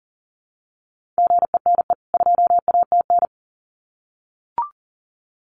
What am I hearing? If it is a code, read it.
Morse code, heard as a keyed tone, about 40 wpm, 700 Hz. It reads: GENE 2ATN
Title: CWT Exchange 40wpm